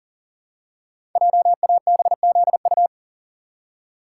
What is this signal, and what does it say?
Morse code, heard as a keyed tone, about 39 wpm, 690 Hz. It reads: JABZU
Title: Code Groups - 5 Letters - 40wpm